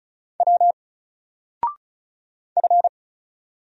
Morse code, heard as a keyed tone, about 36 wpm, 700 Hz. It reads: W F
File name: Single-Letter-Number-35wpm